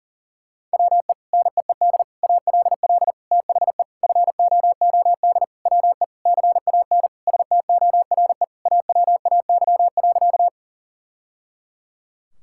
Morse code, heard as a keyed tone, about 37 wpm, 700 Hz. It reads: WE NEED ALL THE FOOD WE CAN STORE AWAY.
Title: Sentences from Top 1000 Words - Review Part 5 - 40wpm